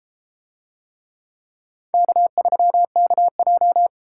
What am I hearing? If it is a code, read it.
Morse code, heard as a keyed tone, about 31 wpm, 700 Hz. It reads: K3KJ